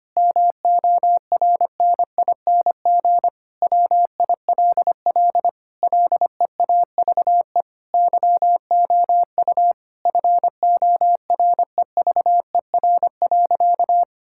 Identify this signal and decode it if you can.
Morse code, heard as a keyed tone, about 25 wpm, 700 Hz. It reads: MORNING WILL LEAVE YOU FOREVER.
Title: Sentences from Top 1600 Words 25wpm